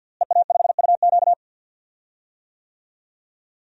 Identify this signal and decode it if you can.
Morse code, heard as a keyed tone, about 52 wpm, 700 Hz. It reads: EA5UQ
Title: DXCC-100 Call Signs - 5 Characters - Repeated 2x - 50wpm